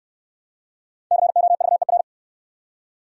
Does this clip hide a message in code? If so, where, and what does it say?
Yes: Morse code from 1.1 to 2.0 s, heard as a keyed tone, about 69 wpm, 700 Hz. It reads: COVER